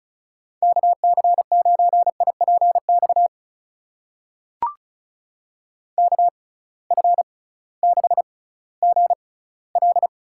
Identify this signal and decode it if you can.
Morse code, heard as a keyed tone, about 36 wpm, spaced as for 19 wpm, 700 Hz. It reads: KC9IPX KF6GL
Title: USA Call Signs - 6 Characters - 6x Character Spacing - 35wpm